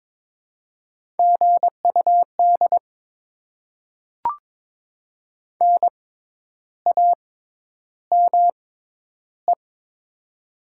Morse code, heard as a keyed tone, about 22 wpm, spaced as for 10 wpm, 690 Hz. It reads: GUD NAME
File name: Top 20 QSO Elements - 6x Character Spacing - 22wpm